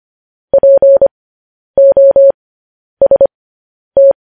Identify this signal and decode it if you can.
Morse code, heard as a keyed tone, about 25 wpm, spaced as for 10 wpm, 550 Hz. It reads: POST
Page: news-headlines-in-morse-code-at-10-wpm